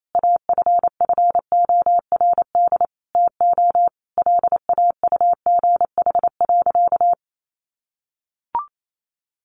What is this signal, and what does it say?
Morse code, heard as a keyed tone, about 28 wpm, 700 Hz. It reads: AFFORD TO LAUGH.
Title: Sentences from Top 1100 Words 28wpm